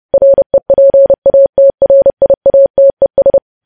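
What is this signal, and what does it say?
Morse code, heard as a keyed tone, about 30 wpm, 550 Hz. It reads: REPATRIATES
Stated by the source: Morse code news headlines sent at 30 words per minute.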